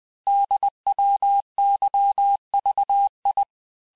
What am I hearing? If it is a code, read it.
Morse code, heard as a keyed tone, about 20 wpm, 800 Hz. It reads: DWYVI
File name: cours-du-lundi files / cours-1584-teste-chiffre-vitesse-19mpm